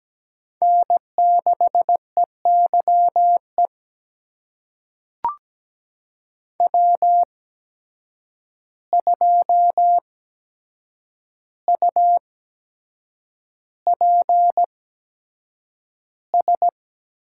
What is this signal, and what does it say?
Morse code, heard as a keyed tone, about 17 wpm, spaced as for 7 wpm, 700 Hz. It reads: N6EYE W2UPS
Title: USA Call Signs - 5 Characters - 8x Character Spacing - 17wpm